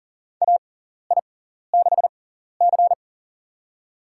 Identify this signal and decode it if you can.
Morse code, heard as a keyed tone, about 40 wpm, spaced as for 14 wpm, 700 Hz. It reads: AI6C
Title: USA Call Signs - 6x Character Spacing - 40wpm